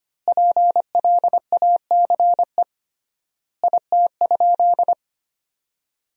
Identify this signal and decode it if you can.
Morse code, heard as a keyed tone, about 25 wpm, 700 Hz. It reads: PLACE IT?
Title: 3x Word-Spacing - Sentences from Top 200 Words - 25wpm